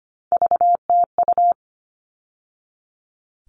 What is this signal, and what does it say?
Morse code, heard as a keyed tone, about 25 wpm, 700 Hz. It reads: VTU
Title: All Cut Numbers - 3-Digit 25wpm